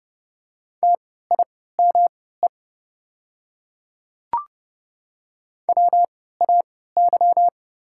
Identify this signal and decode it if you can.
Morse code, heard as a keyed tone, about 31 wpm, spaced as for 17 wpm, 690 Hz. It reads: TIME WAY
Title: 3x Character Spacing - Top 100 Words - 30wpm